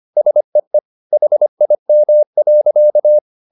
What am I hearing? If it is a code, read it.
Morse code, heard as a keyed tone, about 25 wpm, 600 Hz. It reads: SEE HIM.